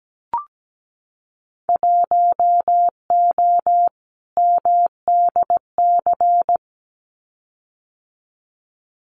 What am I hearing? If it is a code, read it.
Morse code, heard as a keyed tone, about 17 wpm, 700 Hz. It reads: 1O MDC
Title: Winter Field Day Exchanges 17wpm